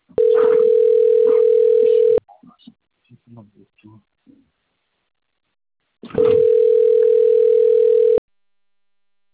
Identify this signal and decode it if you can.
Ringback tone (North America), 440 + 480 Hz, with speech.
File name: rg-1600-unknown- - - .49291